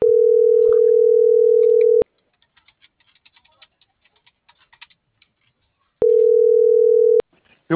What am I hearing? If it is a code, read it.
Ringback tone (North America), 440 + 480 Hz.